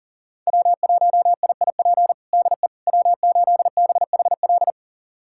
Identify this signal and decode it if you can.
Morse code, heard as a keyed tone, about 40 wpm, 700 Hz. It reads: W1IIP DE W8BHL
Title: USA Callsign DE Callsign - 5 Characters - 40wpm